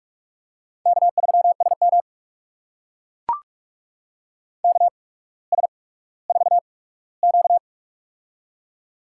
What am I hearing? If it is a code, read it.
Morse code, heard as a keyed tone, about 44 wpm, spaced as for 19 wpm, 700 Hz. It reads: K3SM KS4Q
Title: USA Call Signs - 4 Characters - 8x Character Spacing - 45wpm